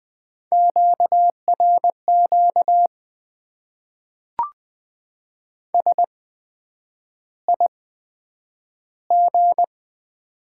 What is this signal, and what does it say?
Morse code, heard as a keyed tone, about 20 wpm, spaced as for 9 wpm, 700 Hz. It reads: QRQ SIG
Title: Top 90 QSO Elements - 8x Character Spacing - 20wpm